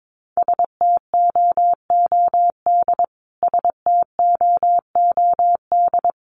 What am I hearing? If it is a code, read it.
Morse code, heard as a keyed tone, about 22 wpm, 700 Hz. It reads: STOOD STOOD
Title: Top 500 Words Repeated 22wpm